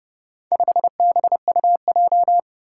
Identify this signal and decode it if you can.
Morse code, heard as a keyed tone, about 31 wpm, 700 Hz. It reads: 5BUJ
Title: Four Challenging Letter-Number Combinations 30wpm